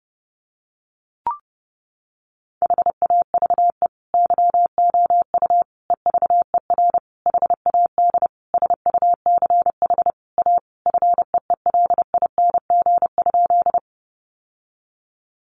Morse code, heard as a keyed tone, about 30 wpm, 700 Hz. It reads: HAVE YOU EVER HAD SUCH A FEELING?
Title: Sentences from Top 1200 Words 30wpm